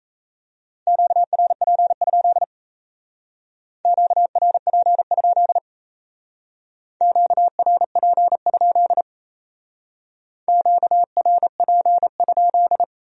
Morse code, heard as a keyed tone, about 36 wpm, 700 Hz. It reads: QRP? QRP? QRP? QRP?